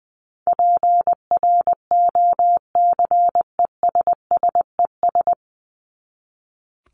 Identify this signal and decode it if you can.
Morse code, heard as a keyed tone, about 19 wpm, 700 Hz. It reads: PROCESSES
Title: ES:S Suffix - Top 100 Words with Base Words 20wpm